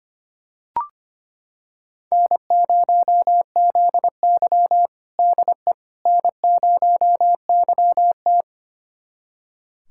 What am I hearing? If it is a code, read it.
Morse code, heard as a keyed tone, about 25 wpm, 700 Hz. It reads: N0ZY DE N0YT
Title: USA Callsign DE Callsign - 4 Characters - 25wpm